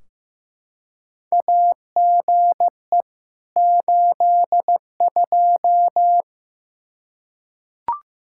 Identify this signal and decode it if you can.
Morse code, heard as a keyed tone, about 15 wpm, 700 Hz. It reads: AGE 82
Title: Practice QSO - Exchange 3 - More Info - 15wpm